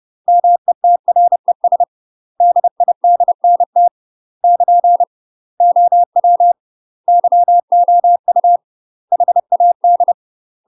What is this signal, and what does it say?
Morse code, heard as a keyed tone, about 30 wpm, 700 Hz. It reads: METRES DIDNT <KN> OW YOU HAD
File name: UK MASTER QSOs TRAINING / 30WPM / UK MASTER QSOs 0019